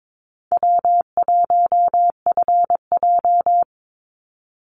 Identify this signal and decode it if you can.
Morse code, heard as a keyed tone, about 22 wpm, 700 Hz. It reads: W1FJ
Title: Easy Call Signs Repeated 3x - 22wpm